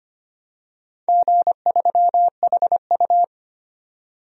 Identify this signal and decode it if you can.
Morse code, heard as a keyed tone, about 25 wpm, 700 Hz. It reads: G3HU
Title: DXCC-100 Call Signs - 4 Characters - 25wpm